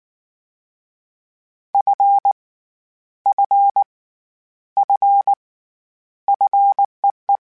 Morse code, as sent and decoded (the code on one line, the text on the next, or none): ..-. / ..-. / ..-. / ..-. .
F F F FEE